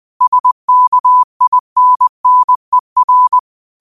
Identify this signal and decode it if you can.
Morse code, heard as a keyed tone, about 20 wpm, 1000 Hz. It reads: SKINNER